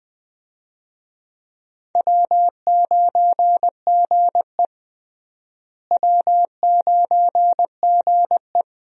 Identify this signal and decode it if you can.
Morse code, heard as a keyed tone, about 20 wpm, 700 Hz. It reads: W9GE W9GE
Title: USA Call Signs - 4 Characters - Repeated 2x - 20wpm